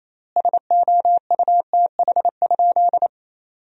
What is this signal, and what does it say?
Morse code, heard as a keyed tone, about 28 wpm, 700 Hz. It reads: SOUTH?
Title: 3x Word-Spacing - Sentences from Top 400 Words - 28wpm